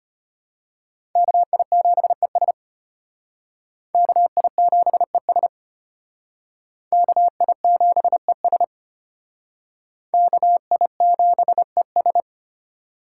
Morse code, heard as a keyed tone, about 32 wpm, 700 Hz. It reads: KI7ES KI7ES KI7ES KI7ES